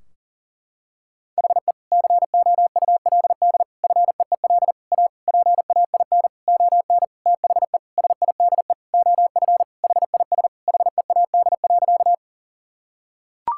Morse code, heard as a keyed tone, about 40 wpm, 700 Hz. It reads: HE COULD FEEL A PAIN ON THE SIDE OF HIS HEAD.
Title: Sentences from Top 1600 Words 40wpm